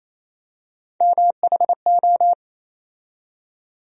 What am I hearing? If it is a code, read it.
Morse code, heard as a keyed tone, about 28 wpm, 700 Hz. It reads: MHO
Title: USA Call Sign Postfixes - 3 Letters 28wpm